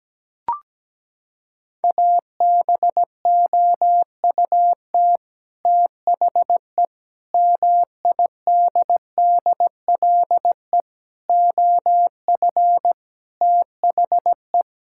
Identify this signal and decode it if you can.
Morse code, heard as a keyed tone, about 17 wpm, 700 Hz. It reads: ABOUT THE MIDDLE OF THE
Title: Sentences from Top 1000 Words - Review Part 6 - 17wpm